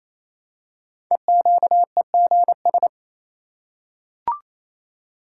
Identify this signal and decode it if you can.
Morse code, heard as a keyed tone, about 29 wpm, 700 Hz. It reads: EQEGS
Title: Code Groups - 5 Letters - 28wpm